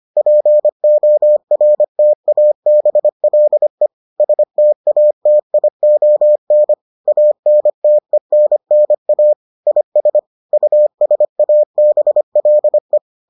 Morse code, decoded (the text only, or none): PORTABLE STATION ANTENNA IS USABLE